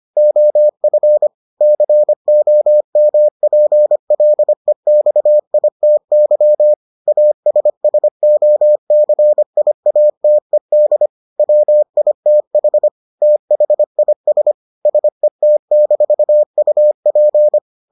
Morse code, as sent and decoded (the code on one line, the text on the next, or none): --- ..-. / -.-. --- -- .--. .-.. -..- .. - -.-- / .- ... ... --- -.-. .. .- - . -.. / .-- .. - .... / - .... .. ... / ... - -....- ..- .--.
OF COMPLEXITY ASSOCIATED WITH THIS SET-UP